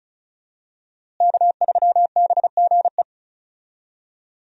Morse code, heard as a keyed tone, about 34 wpm, 700 Hz. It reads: K3BGE
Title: USA Call Signs - 5 Characters - Repeated 2x - 35wpm